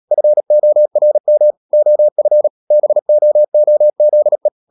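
Morse code, heard as a keyed tone, about 37 wpm, 600 Hz. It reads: FORM OF BOOZE
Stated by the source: Morse code practice at 35–39 wpm sending English text of Ken Follett's novel Never, from CW med Gnister.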